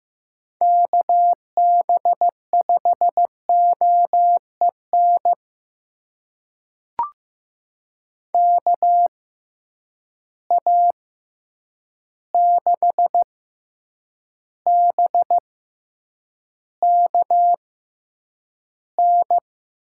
Morse code, heard as a keyed tone, about 15 wpm, spaced as for 8 wpm, 700 Hz. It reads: KB5OEN KA6BKN
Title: USA Call Signs - 6 Characters - 6x Character Spacing - 15wpm